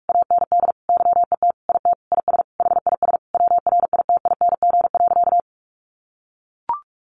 Morse code, heard as a keyed tone, about 44 wpm, 700 Hz. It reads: AND YET IT IS HIS WRITING.
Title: Sentences from Top 800 Words 45wpm